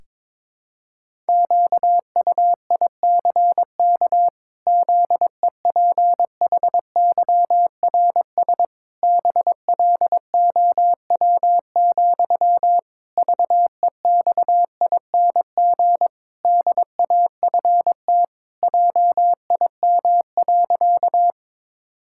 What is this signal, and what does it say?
Morse code, heard as a keyed tone, about 22 wpm, 700 Hz. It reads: QUICK ZEPHYRS BLOW, VEXING DAFT JIM.
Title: Pangram Challenge v2 - 22wpm